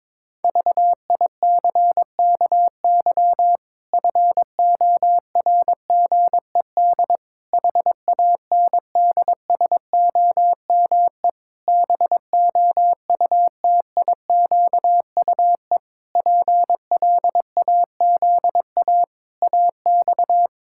Morse code, heard as a keyed tone, about 21 wpm, 700 Hz. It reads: VICKY FORGED HANDSOME BOUTIQUE PLAZA AX
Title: Pangram Challenge v2 - 22wpm